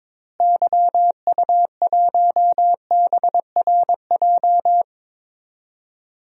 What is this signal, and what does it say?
Morse code, heard as a keyed tone, about 22 wpm, 700 Hz. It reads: YU1BRJ
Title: DXCC-100 Call Signs - 6 Characters - Repeated 2x - 22wpm